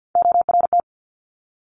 Morse code, heard as a keyed tone, about 50 wpm, 700 Hz. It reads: ORT
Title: Top Three Letter Combinations in Words 50wpm